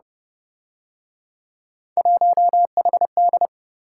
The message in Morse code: 1HD

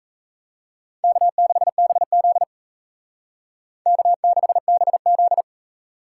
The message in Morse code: K6BZ K6BZ

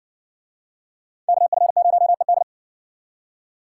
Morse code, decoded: KF0ED

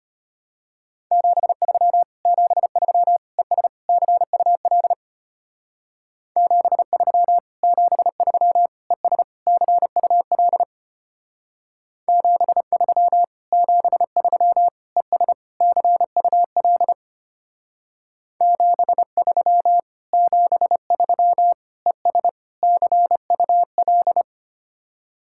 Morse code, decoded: 73 73 ES CUL 73 73 ES CUL 73 73 ES CUL 73 73 ES CUL

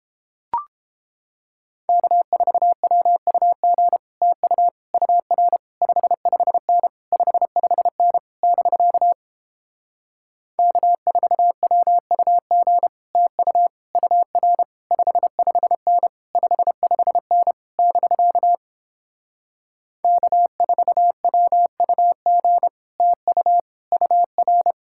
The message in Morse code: K4WUG TU UR 55N 55N <BK> K4WUG TU UR 55N 55N <BK> K4WUG TU UR

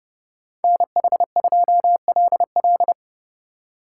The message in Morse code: NH2LL

